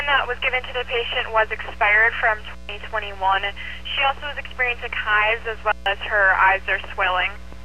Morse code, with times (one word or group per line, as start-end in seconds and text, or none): none